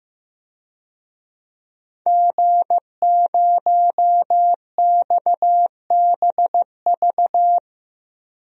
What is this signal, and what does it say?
Morse code, heard as a keyed tone, about 15 wpm, 700 Hz. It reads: G0XBV
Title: DXCC-100 Call Signs - 5 Characters - Repeated 3x - 15wpm